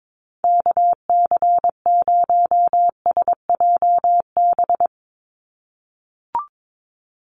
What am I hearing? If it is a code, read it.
Morse code, heard as a keyed tone, about 22 wpm, 700 Hz. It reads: KC0SJB
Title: USA Call Signs 22wpm